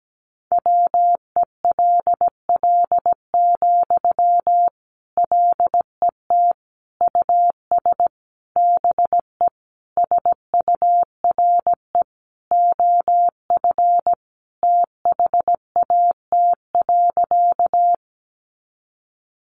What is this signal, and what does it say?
Morse code, heard as a keyed tone, about 17 wpm, 700 Hz. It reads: WELL, LET US BE SURE OF THAT.
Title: Sentences from Top 300 Words 17wpm